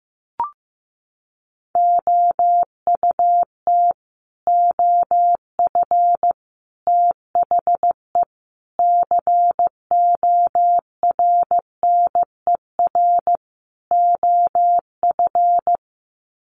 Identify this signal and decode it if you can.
Morse code, heard as a keyed tone, about 15 wpm, 700 Hz. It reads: OUT OF THE CORNER OF
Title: Sets of 5 Words 15wpm